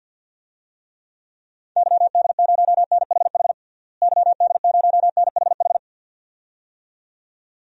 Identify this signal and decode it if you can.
Morse code, heard as a keyed tone, about 52 wpm, 700 Hz. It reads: YD0NHH YD0NHH
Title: DXCC-100 Call Signs - 6 Characters - Repeated 2x - 50wpm